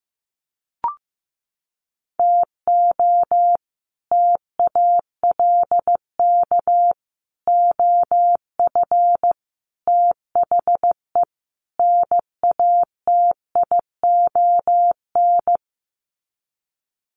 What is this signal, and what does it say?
Morse code, heard as a keyed tone, about 15 wpm, 700 Hz. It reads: TO TALK OF THE NATION
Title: Sets of 5 Words 15wpm